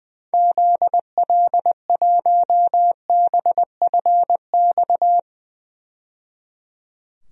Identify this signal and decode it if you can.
Morse code, heard as a keyed tone, about 21 wpm, 700 Hz. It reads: ZL1BFX
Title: DXCC-100 Call Signs - 6 Characters - 20wpm